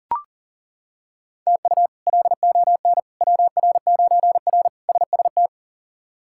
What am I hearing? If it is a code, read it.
Morse code, heard as a keyed tone, about 37 wpm, 700 Hz. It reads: TU LON WR9R SST
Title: K1USN SST QSO Practice - 40wpm